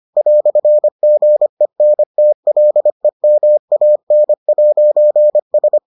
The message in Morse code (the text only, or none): "GENTLEMAN'S